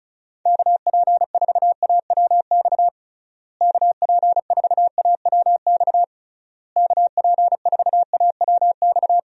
Morse code, heard as a keyed tone, about 36 wpm, 700 Hz. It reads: KP4AWX KP4AWX KP4AWX
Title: DXCC-100 Call Signs - 6 Characters - Repeated 3x - 35wpm